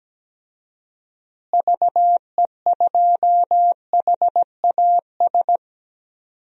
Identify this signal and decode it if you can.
Morse code, heard as a keyed tone, about 17 wpm, 700 Hz. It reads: VE2HAS
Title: DXCC-100 Call Signs - 6 Characters - Repeated 3x - 17wpm